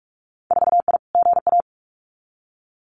Morse code, heard as a keyed tone, about 46 wpm, 700 Hz. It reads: 4I GA
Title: Winter Field Day Exchanges Encore 45wpm